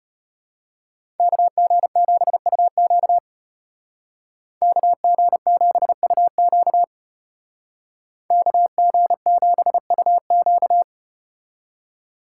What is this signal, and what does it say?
Morse code, heard as a keyed tone, about 34 wpm, 700 Hz. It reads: KG7UQ KG7UQ KG7UQ